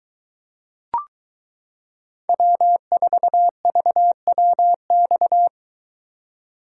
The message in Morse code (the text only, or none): W4VWX